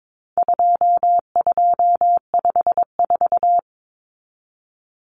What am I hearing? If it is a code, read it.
Morse code, heard as a keyed tone, about 22 wpm, 700 Hz. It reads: 2254
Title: Common Cut Numbers - 4-Digit 22wpm